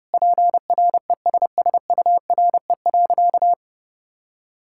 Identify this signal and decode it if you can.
Morse code, heard as a keyed tone, about 30 wpm, 700 Hz. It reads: PRESSURE.
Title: Sentences from Top 1000 Words - Review Part 6 - 30wpm